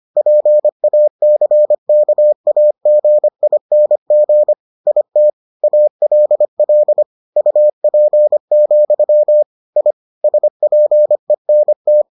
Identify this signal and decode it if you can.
Morse code, heard as a keyed tone, about 25 wpm, 600 Hz. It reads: PACKAGING IT ALL UP, I SPENT